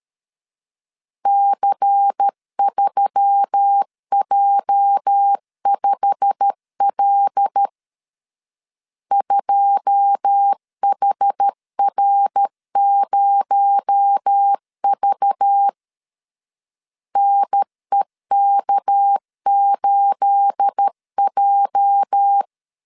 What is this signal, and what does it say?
Morse code, heard as a keyed tone, about 13 wpm, 800 Hz. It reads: C3J5L 2HR0V NEK8J
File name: CW / DET / 1 9 1500C S B 12wpm 3819